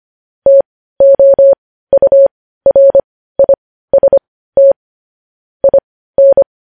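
Morse code, heard as a keyed tone, about 25 wpm, spaced as for 15 wpm, 550 Hz. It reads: TOURIST IN
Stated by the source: Morse news headlines practice at 15 wpm.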